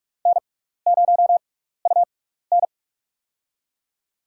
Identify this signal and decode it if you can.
Morse code, heard as a keyed tone, about 44 wpm, spaced as for 15 wpm, 700 Hz. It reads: N0UN
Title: USA Call Signs - 4 Characters - 6x Character Spacing - 45wpm